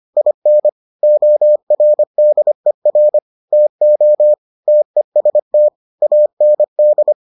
Morse code, heard as a keyed tone, about 25 wpm, 600 Hz. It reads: IN ORDER TO TEST AND